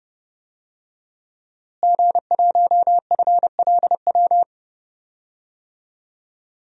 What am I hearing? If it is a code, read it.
Morse code, heard as a keyed tone, about 29 wpm, 700 Hz. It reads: G1FLW
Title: DXCC-100 Call Signs - 5 Characters - Repeated 3x - 30wpm